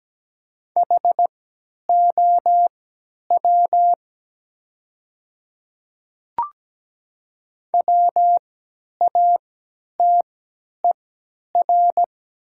Morse code, heard as a keyed tone, about 17 wpm, spaced as for 10 wpm, 700 Hz. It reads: HOW WATER
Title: 3x Character Spacing - Top 100 Words - 17wpm